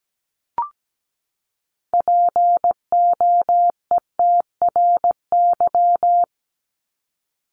Morse code, heard as a keyed tone, about 17 wpm, 700 Hz. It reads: POETRY
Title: Top 1751-2000 Words 17wpm